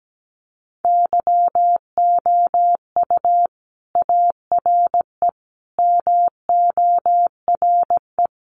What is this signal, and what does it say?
Morse code, heard as a keyed tone, about 17 wpm, 700 Hz. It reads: YOU ARE MORE
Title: Sentences from Top 1300 Words 17wpm